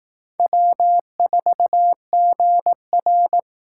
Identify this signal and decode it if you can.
Morse code, heard as a keyed tone, about 18 wpm, 700 Hz. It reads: W4GR
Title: Speed Racing - USA 1x2 Call Signs 15wpm